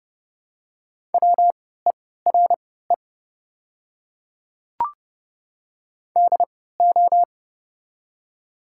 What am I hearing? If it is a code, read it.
Morse code, heard as a keyed tone, about 29 wpm, spaced as for 17 wpm, 690 Hz. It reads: WERE DO
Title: 3x Character Spacing - Top 200 Words - 30wpm